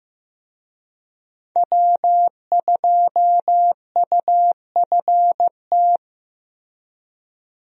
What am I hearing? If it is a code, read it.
Morse code, heard as a keyed tone, about 15 wpm, 700 Hz. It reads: W2UFT
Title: USA Call Signs - 5 Characters - Repeated 3x - 15wpm